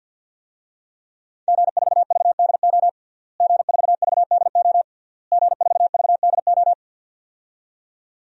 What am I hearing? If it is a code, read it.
Morse code, heard as a keyed tone, about 51 wpm, 700 Hz. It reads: G4VDO G4VDO G4VDO
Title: DXCC-100 Call Signs - 5 Characters - Repeated 3x - 50wpm